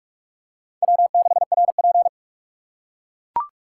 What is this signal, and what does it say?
Morse code, heard as a keyed tone, about 42 wpm, 700 Hz. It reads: W6RP